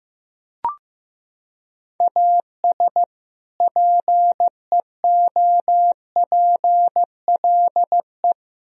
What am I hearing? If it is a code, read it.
Morse code, heard as a keyed tone, about 16 wpm, 700 Hz. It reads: AS PEOPLE
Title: Sentences from Top 1000 Words - Review Part 4 - 15wpm